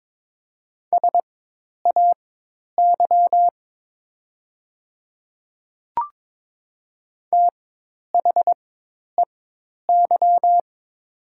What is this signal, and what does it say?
Morse code, heard as a keyed tone, about 22 wpm, spaced as for 10 wpm, 700 Hz. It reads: SAY THEY
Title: 4x Character Spacing - Top 500 Words - 22wpm